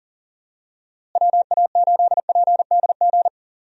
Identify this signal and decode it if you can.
Morse code, heard as a keyed tone, about 40 wpm, 700 Hz. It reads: WA8PDG
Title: USA Call Signs - 6 Characters - Repeated 3x - 40wpm